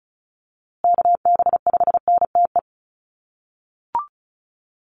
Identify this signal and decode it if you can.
Morse code, heard as a keyed tone, about 34 wpm, 700 Hz. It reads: KB5NTE
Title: USA Call Signs Repeated 3x 35wpm